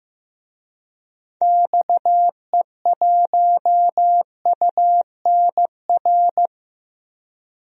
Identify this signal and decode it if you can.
Morse code, heard as a keyed tone, about 15 wpm, 700 Hz. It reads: XE1UNR
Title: DXCC-100 Call Signs - 6 Characters - 15wpm